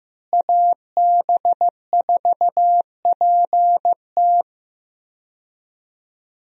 Morse code, heard as a keyed tone, about 15 wpm, 700 Hz. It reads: AB4PT
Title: USA Call Signs - 5 Characters - Repeated 3x - 15wpm